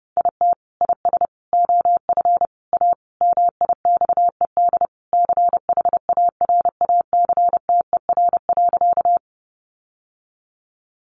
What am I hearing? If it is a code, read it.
Morse code, heard as a keyed tone, about 30 wpm, 700 Hz. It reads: IT IS OF A MIXED CHARACTER.
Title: Sentences from Top 1500 Words 30wpm